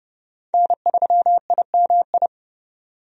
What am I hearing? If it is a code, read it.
Morse code, heard as a keyed tone, about 30 wpm, 700 Hz. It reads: N3IMI
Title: USA Call Signs - 5 Characters - 30wpm